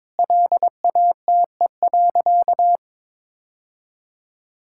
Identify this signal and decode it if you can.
Morse code, heard as a keyed tone, about 22 wpm, 700 Hz. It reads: LATE.
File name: practice / 22wpm / Sentences with 3-Word Phrases - 22wpm